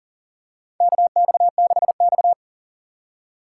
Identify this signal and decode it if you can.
Morse code, heard as a keyed tone, about 40 wpm, 690 Hz. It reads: KX6X
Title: K1USN SST QSO Practice - 40wpm